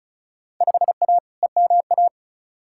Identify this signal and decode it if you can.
Morse code, heard as a keyed tone, about 36 wpm, 700 Hz. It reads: 5A EMA
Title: Field Day Exchanges - 35wpm